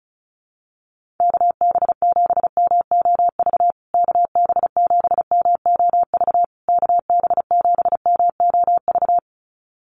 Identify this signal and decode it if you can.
Morse code, heard as a keyed tone, about 34 wpm, 700 Hz. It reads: KB7MOV KB7MOV KB7MOV
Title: USA Call Signs Repeated 3x 35wpm